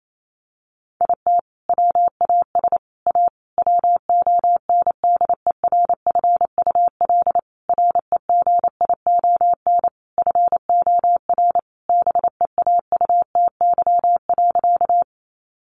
Morse code, heard as a keyed tone, about 28 wpm, 700 Hz. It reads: IT WAS A WONDERFUL REGION FOR BEAUTY.
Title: Sentences from Top 1500 Words 28wpm